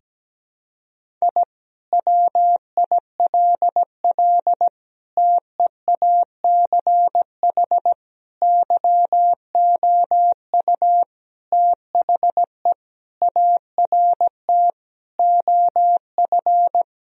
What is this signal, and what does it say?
Morse code, heard as a keyed tone, about 17 wpm, 700 Hz. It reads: I WILL TEACH YOU THE ART OF
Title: Sentences from Top 1000 Words - Review Part 3 - 17wpm